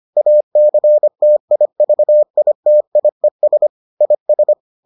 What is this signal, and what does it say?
Morse code, heard as a keyed tone, about 25 wpm, 600 Hz. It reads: ACTIVITIES IS